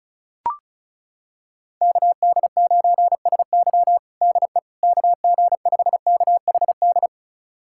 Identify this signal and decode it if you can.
Morse code, heard as a keyed tone, about 36 wpm, 700 Hz. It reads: KD9SY DE KG5KHD
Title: USA Callsign DE Callsign 35wpm